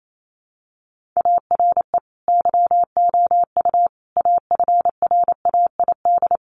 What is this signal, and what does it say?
Morse code, heard as a keyed tone, about 28 wpm, 700 Hz. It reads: ARE YOU AFRAID
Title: Sentences from Top 1500 Words 28wpm